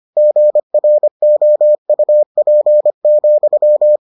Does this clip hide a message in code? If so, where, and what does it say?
Yes: Morse code from 0.2 to 4.0 s, heard as a keyed tone, about 25 wpm, 600 Hz. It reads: GROUP,